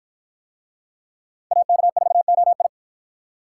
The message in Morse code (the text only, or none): AK4OI